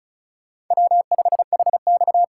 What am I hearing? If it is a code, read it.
Morse code, heard as a keyed tone, about 36 wpm, 700 Hz. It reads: W5HX
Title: USA Call Signs - 4 Characters - 2x Character Spacing - 35wpm